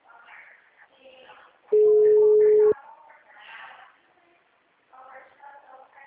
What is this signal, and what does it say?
Ringback tone (Europe, 425 Hz), with speech.
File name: out- -2045- - - .31296